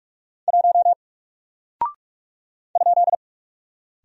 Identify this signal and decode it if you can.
Morse code, heard as a keyed tone, about 45 wpm, 700 Hz. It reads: 1 ?
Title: Single Character (Letter:Number:Punctuation:Prosign) 45wpm